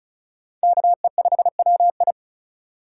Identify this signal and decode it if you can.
Morse code, heard as a keyed tone, about 36 wpm, 700 Hz. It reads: KE5WI
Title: USA Call Signs - 5 Characters - 4x Character Spacing - 35wpm